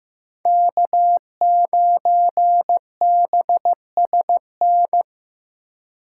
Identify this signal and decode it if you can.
Morse code, heard as a keyed tone, about 18 wpm, 700 Hz. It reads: K9BSN
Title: Speed Racing - USA 1x3 Call Signs 15wpm